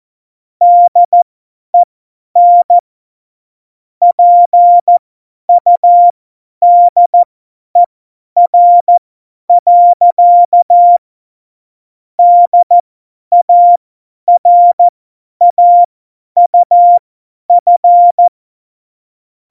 Morse code, heard as a keyed tone, about 14 wpm, spaced as for 10 wpm, 700 Hz. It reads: DEN PUDER. DARAUF
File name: Morsekurs / Witze / Witz 03. Oktober 17, Tempo 50 F70